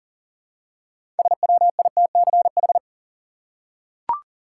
Morse code, heard as a keyed tone, about 40 wpm, 700 Hz. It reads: SWITCH
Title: Six Letter Words - 40wpm